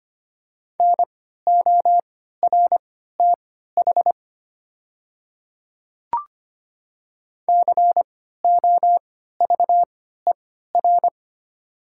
Morse code, heard as a keyed tone, about 25 wpm, spaced as for 14 wpm, 690 Hz. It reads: NORTH COVER
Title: 3x Character Spacing - Top 500 Words - 25wpm